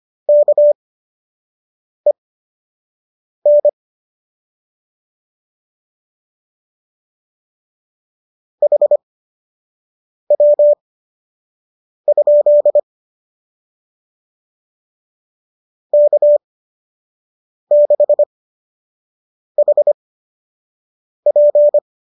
Morse code, heard as a keyed tone, about 25 wpm, spaced as for 6 wpm, 590 Hz. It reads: KEN HW? K6HP
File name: qso003 06